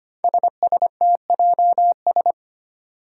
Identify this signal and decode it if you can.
Morse code, heard as a keyed tone, about 25 wpm, 700 Hz. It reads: SSTJS
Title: Five Challenging Letter-Number Combinations 25wpm